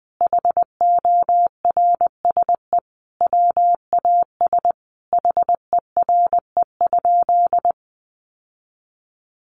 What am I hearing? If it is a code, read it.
Morse code, heard as a keyed tone, about 20 wpm, 700 Hz. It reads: HORSE WAS HERE?
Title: Sentences from Top 1500 Words 20wpm